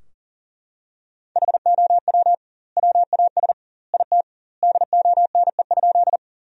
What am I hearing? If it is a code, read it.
Morse code, heard as a keyed tone, about 40 wpm, 700 Hz. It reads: HOW WAS IT DONE?
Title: 2x Word-Spacing - Sentences from Top 500 Words - 40wpm